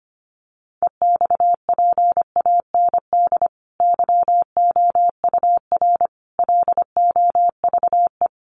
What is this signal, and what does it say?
Morse code, heard as a keyed tone, about 25 wpm, 700 Hz. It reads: EXPAND YOUR LOVE
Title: Sentences from Top 1400 Words 25wpm